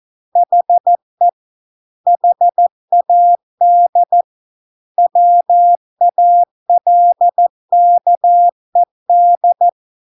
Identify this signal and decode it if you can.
Morse code, heard as a keyed tone, about 14 wpm, 700 Hz. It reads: HE HAD WALKED